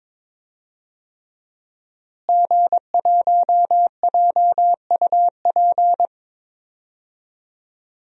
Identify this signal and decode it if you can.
Morse code, heard as a keyed tone, about 22 wpm, 700 Hz. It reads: G1JUP
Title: DXCC-100 Call Signs - 5 Characters - Repeated 3x - 22wpm